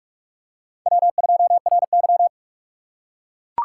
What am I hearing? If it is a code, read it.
Morse code, heard as a keyed tone, about 44 wpm, 700 Hz. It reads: W2RY